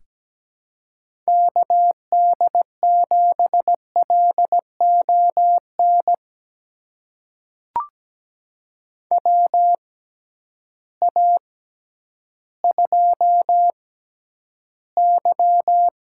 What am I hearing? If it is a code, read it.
Morse code, heard as a keyed tone, about 17 wpm, spaced as for 10 wpm, 700 Hz. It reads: KD7LON WA2Y
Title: USA Call Signs - 6 Characters - 6x Character Spacing - 17wpm